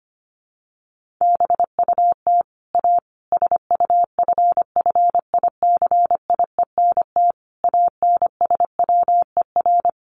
Morse code, heard as a keyed tone, about 25 wpm, 700 Hz. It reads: BUT A SUFFICIENT ANSWER
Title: Sentences from Top 1400 Words 25wpm